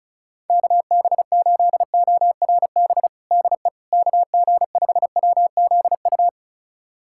Morse code, heard as a keyed tone, about 36 wpm, 700 Hz. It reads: KB8ORB DE KG5WZU
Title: USA Callsign DE Callsign - 6 Characters - 35wpm